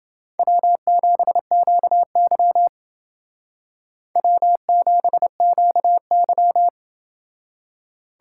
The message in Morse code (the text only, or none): W7QY W7QY